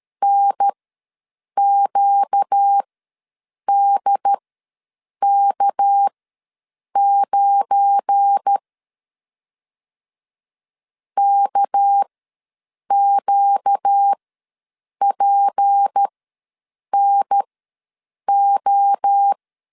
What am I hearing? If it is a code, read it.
Morse code, heard as a keyed tone, about 13 wpm, spaced as for 7 wpm, 800 Hz. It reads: NQDK9 KQPNO
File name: LEZIONI / 8 YJ45 / 8 Y ws 12wpm 3409